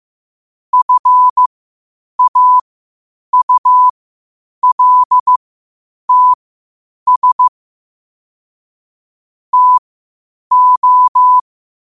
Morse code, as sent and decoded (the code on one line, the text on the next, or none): ..-. .- ..- .-.. - ... / - ---
FAULTS TO